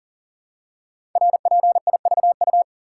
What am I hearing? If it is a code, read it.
Morse code, heard as a keyed tone, about 38 wpm, 700 Hz. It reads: RPIVU
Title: Code Groups - 5 Letters - 40wpm